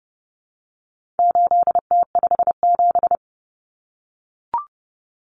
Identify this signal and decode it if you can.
Morse code, heard as a keyed tone, about 30 wpm, 700 Hz. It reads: 8T57
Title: Common Cut Numbers - 4-Digit 30wpm